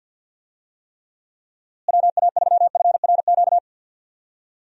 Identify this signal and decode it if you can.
Morse code, heard as a keyed tone, about 52 wpm, 700 Hz. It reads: WA3FRQ
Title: USA Call Signs - 6 Characters - Repeated 2x - 50wpm